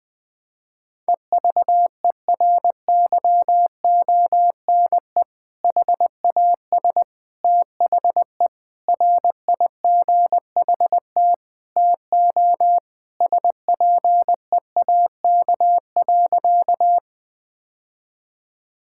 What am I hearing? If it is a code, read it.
Morse code, heard as a keyed tone, about 20 wpm, 700 Hz. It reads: EVERYONE HAS THE RIGHT TO SPEAK.
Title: Sentences with 3-Word Phrases - Spoken Beginning - 20wpm